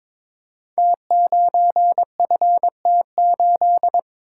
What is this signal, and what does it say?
Morse code, heard as a keyed tone, about 22 wpm, 700 Hz. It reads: T9FT8
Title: Code Groups - 5 Letter:Numbers - 22wpm